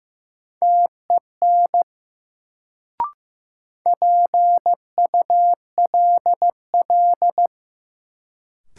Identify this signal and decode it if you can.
Morse code, heard as a keyed tone, about 15 wpm, 700 Hz. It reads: TEN PULL
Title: Sentences from Top 400 Words - New Word Drill - 15wpm